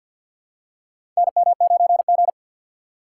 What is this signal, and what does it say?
Morse code, heard as a keyed tone, about 47 wpm, 700 Hz. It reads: NM9G
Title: USA Call Signs - 4 Characters - 6x Character Spacing - 50wpm